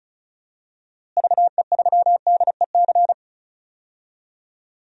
Morse code, heard as a keyed tone, about 36 wpm, 700 Hz. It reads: VE3DEC
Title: DXCC-100 Call Signs - 6 Characters - Repeated 3x - 35wpm